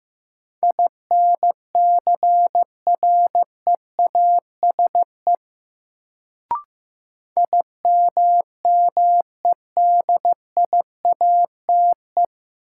Morse code, heard as a keyed tone, about 15 wpm, 700 Hz. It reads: INCREASE IMMEDIATE
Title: Sentences from Top 1200 Words - New Word Drill Alt - 15wpm